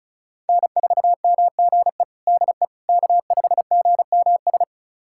Morse code, heard as a keyed tone, about 34 wpm, 700 Hz. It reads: N4MGE DE K5GMS
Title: USA Callsign DE Callsign - 5 Characters - 35wpm